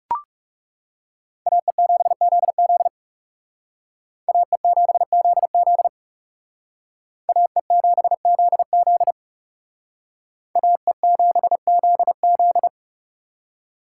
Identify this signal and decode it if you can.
Morse code, heard as a keyed tone, about 38 wpm, 700 Hz. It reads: AE7ZZ AE7ZZ AE7ZZ AE7ZZ